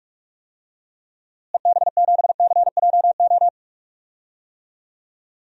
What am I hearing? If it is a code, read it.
Morse code, heard as a keyed tone, about 46 wpm, 700 Hz. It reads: EB7CJO
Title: DXCC-100 Call Signs - 6 Characters - Repeated 3x - 45wpm